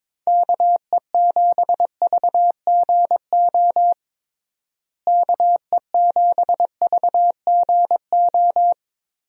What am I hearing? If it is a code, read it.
Morse code, heard as a keyed tone, about 22 wpm, 700 Hz. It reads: KE7VGO KE7VGO
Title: USA Call Signs - 6 Characters - Repeated 3x - 22wpm